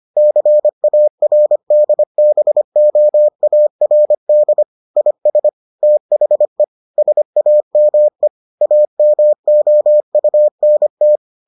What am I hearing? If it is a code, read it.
Morse code, heard as a keyed tone, about 25 wpm, 600 Hz. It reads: CARDBOARD IS THE SAME AMOUNT